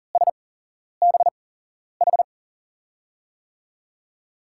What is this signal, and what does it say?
Morse code, heard as a keyed tone, about 40 wpm, spaced as for 11 wpm, 700 Hz. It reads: SBH
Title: USA Call Signs - 8x Character Spacing - 40wpm